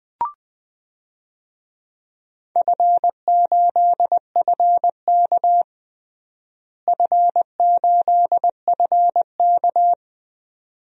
Morse code, heard as a keyed tone, about 20 wpm, 700 Hz. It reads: F8FK F8FK
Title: DXCC-100 Call Signs - 4 Characters - Repeated 2x - 20wpm